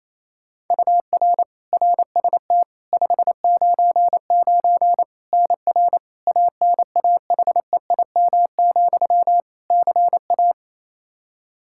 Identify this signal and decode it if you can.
Morse code, heard as a keyed tone, about 28 wpm, 700 Hz. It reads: UR RST 599 NR ANAHEIM, CA
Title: Practice QSO - Exchange 1 - Beginning - 28wpm